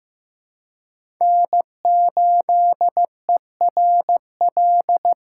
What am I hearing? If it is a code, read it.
Morse code, heard as a keyed tone, about 15 wpm, 700 Hz. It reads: N8ERL
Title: USA Call Signs - 5 Characters - 2x Character Spacing - 15wpm